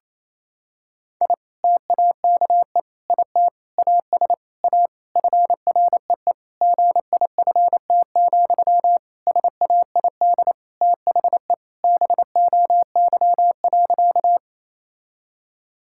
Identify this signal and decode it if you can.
Morse code, heard as a keyed tone, about 28 wpm, 700 Hz. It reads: I TAKE IT AS A FREE GIFT, SAID THE BOY.
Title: Sentences from Top 1000 Words - Review Part 6 - 28wpm